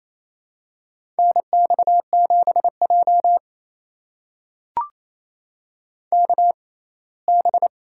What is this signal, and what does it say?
Morse code, heard as a keyed tone, about 28 wpm, spaced as for 19 wpm, 700 Hz. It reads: NX7J KB
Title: USA Call Signs - 6x Character Spacing - 28wpm